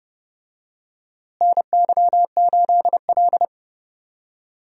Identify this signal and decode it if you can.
Morse code, heard as a keyed tone, about 30 wpm, 700 Hz. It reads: NY8L
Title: USA Call Signs - 4 Characters - Repeated 2x - 30wpm